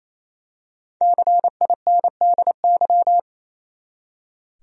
Morse code, heard as a keyed tone, about 27 wpm, 700 Hz. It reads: CINDY
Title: USA Common Names - 28wpm